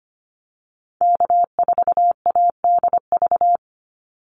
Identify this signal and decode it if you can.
Morse code, heard as a keyed tone, about 25 wpm, 700 Hz. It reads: K4ADV
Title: NA SOTA Activator Callsigns 25wpm